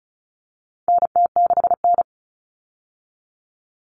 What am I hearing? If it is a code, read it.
Morse code, heard as a keyed tone, about 34 wpm, 700 Hz. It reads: NT6N
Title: All Cut Numbers - 4-Digit 35wpm